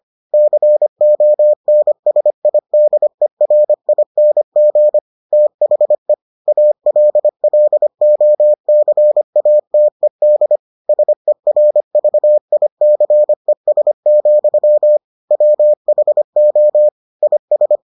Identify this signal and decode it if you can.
Morse code, heard as a keyed tone, about 25 wpm, 600 Hz. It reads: CONSIDERING THE ALLOCATED SERVICES, WHO IS